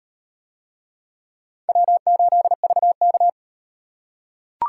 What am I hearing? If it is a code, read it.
Morse code, heard as a keyed tone, about 38 wpm, 700 Hz. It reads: W8VK